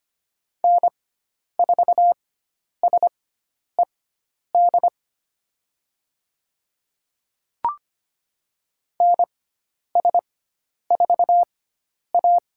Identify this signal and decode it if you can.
Morse code, heard as a keyed tone, about 25 wpm, spaced as for 10 wpm, 700 Hz. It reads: N4SED NS4A
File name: Lesson 16g - Letter c - Call Signs 25wpm